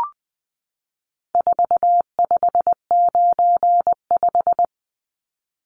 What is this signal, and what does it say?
Morse code, heard as a keyed tone, about 20 wpm, 700 Hz. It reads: 4595
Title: Numbers - 4-Digit 20wpm